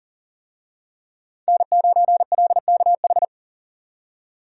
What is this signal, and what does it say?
Morse code, heard as a keyed tone, about 40 wpm, 700 Hz. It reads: N9LKH
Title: USA Call Signs - 5 Characters - 40wpm